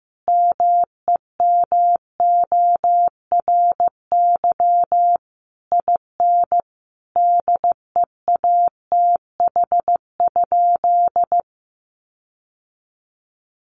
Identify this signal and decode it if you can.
Morse code, heard as a keyed tone, about 15 wpm, 700 Hz. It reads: MEMORY IN DEATH?
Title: Sentences from Top 1300 Words 15wpm